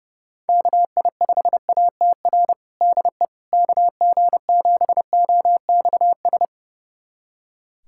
Morse code, heard as a keyed tone, about 30 wpm, 700 Hz. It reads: KI5ATR DE KG7OXS
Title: USA Callsign DE Callsign 30wpm